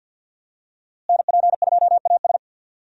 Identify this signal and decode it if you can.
Morse code, heard as a keyed tone, about 51 wpm, 700 Hz. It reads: NP2AS
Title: USA Call Signs - 5 Characters - Repeated 3x - 50wpm